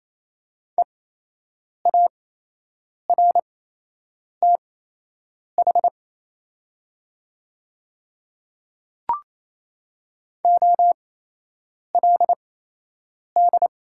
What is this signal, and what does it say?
Morse code, heard as a keyed tone, about 28 wpm, spaced as for 8 wpm, 700 Hz. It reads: EARTH OLD
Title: 8x Character Spacing - Top 500 Words - 28wpm